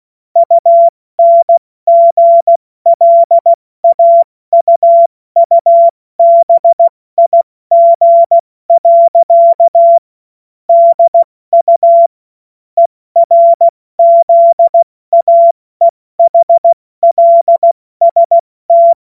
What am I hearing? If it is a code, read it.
Morse code, heard as a keyed tone, about 16 wpm, 680 Hz. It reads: UNGLAUUBIG. DU ERZAEHLST